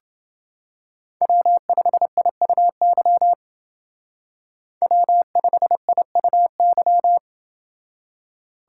Morse code, heard as a keyed tone, about 30 wpm, 700 Hz. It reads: W5IUY W5IUY